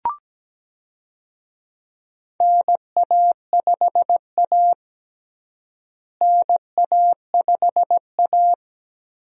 Morse code, heard as a keyed tone, about 17 wpm, 700 Hz. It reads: NA5A NA5A
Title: USA Call Signs - 4 Characters - Repeated 2x - 17wpm